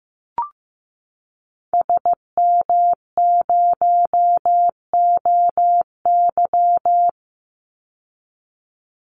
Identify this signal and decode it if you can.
Morse code, heard as a keyed tone, about 15 wpm, 700 Hz. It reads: SM0OY
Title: Call Signs Repeated 2x - 15wpm